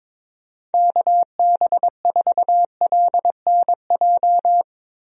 Morse code, heard as a keyed tone, about 22 wpm, 700 Hz. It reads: KB4LNJ
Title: USA Call Signs - 6 Characters - Repeated 3x - 22wpm